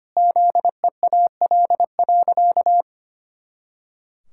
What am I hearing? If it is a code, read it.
Morse code, heard as a keyed tone, about 25 wpm, 700 Hz. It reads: ZEAL.
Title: Pangram Challenge v2 - 25wpm